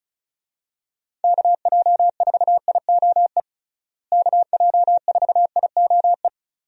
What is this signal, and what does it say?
Morse code, heard as a keyed tone, about 34 wpm, 700 Hz. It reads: KJ4IOE KJ4IOE
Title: USA Call Signs - 6 Characters - Repeated 2x - 35wpm